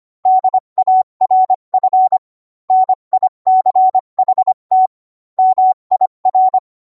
Morse code, heard as a keyed tone, about 25 wpm, 750 Hz. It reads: DARF NICHT MIR